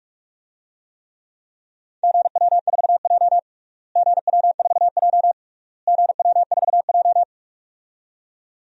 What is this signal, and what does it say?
Morse code, heard as a keyed tone, about 46 wpm, 700 Hz. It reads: GW4J GW4J GW4J
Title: DXCC-100 Call Signs - 4 Characters - Repeated 3x - 45wpm